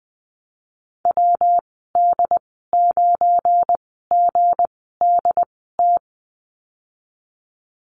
Morse code, heard as a keyed tone, about 20 wpm, spaced as for 14 wpm, 700 Hz. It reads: WD9GDT
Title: 173 Lesson 28g - Number 9 - Call Signs 20wpm